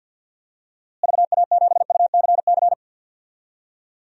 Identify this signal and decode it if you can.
Morse code, heard as a keyed tone, about 50 wpm, 700 Hz. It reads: VA7UCC
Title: DXCC-100 Call Signs - 6 Characters - Repeated 2x - 50wpm